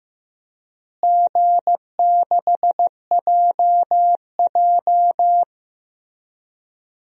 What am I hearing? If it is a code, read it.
Morse code, heard as a keyed tone, about 15 wpm, 700 Hz. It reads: G6JJ
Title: DXCC-100 Call Signs - 4 Characters - Repeated 2x - 15wpm